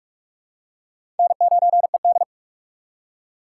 Morse code, heard as a keyed tone, about 46 wpm, 700 Hz. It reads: N9ED
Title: USA Call Signs - 2x Character Spacing - 45wpm